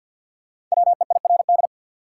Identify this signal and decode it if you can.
Morse code, heard as a keyed tone, about 51 wpm, 700 Hz. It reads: WEIRD
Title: Five Letter Words - 50wpm